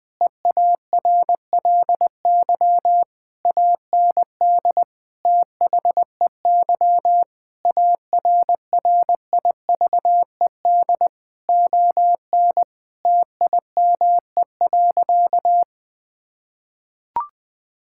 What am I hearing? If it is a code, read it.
Morse code, heard as a keyed tone, about 21 wpm, 700 Hz. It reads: EARLY AND THEY ARRIVED ON TIME.
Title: Sentences with 2-Word Phrases - 20wpm